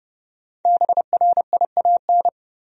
Morse code, heard as a keyed tone, about 29 wpm, 700 Hz. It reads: BRIAN
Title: USA Common Names - Max 5 Letters - 30wpm